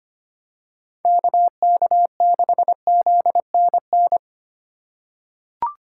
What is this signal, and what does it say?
Morse code, heard as a keyed tone, about 25 wpm, 700 Hz. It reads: KK6ZNN
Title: USA Call Signs - 6 Characters - 2x Character Spacing - 25wpm